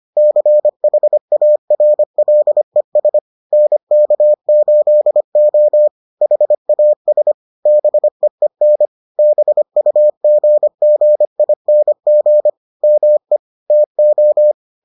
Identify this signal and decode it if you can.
Morse code, heard as a keyed tone, about 25 wpm, 600 Hz. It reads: CHARLES NK8O HAS BEEN BUGGING ME TO